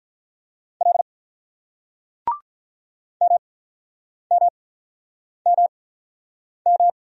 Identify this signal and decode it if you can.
Morse code, heard as a keyed tone, about 46 wpm, spaced as for 9 wpm, 700 Hz. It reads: L MMMM